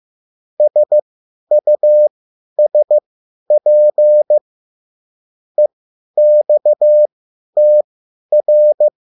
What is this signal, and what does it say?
Morse code, heard as a keyed tone, about 15 wpm, spaced as for 10 wpm, 600 Hz. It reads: SUSP EXTR